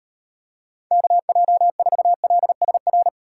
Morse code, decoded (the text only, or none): KJ4LSR